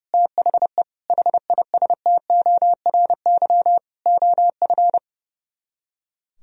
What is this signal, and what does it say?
Morse code, heard as a keyed tone, about 37 wpm, 700 Hz. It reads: THE HISTORY OF
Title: Speed-Racing - Sets of 4 Words 30wpm